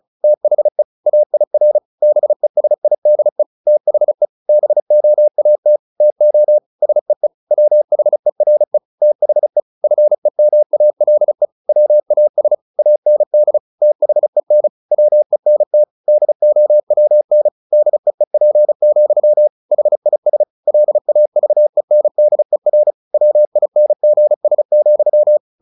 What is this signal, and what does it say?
Morse code, heard as a keyed tone, about 34 wpm, 600 Hz. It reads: THE AIR BESIDE THE BOAT TO SEE WHERE THE FEMALE WAS AND THEN WENT DOWN DEEP, HIS LAVENDER WINGS,
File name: the old man and the sea / the old man and the sea 1 35wpm